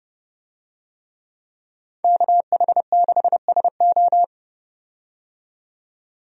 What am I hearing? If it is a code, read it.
Morse code, heard as a keyed tone, about 30 wpm, 700 Hz. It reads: KH6SO
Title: DXCC-100 Call Signs - 5 Characters - 30wpm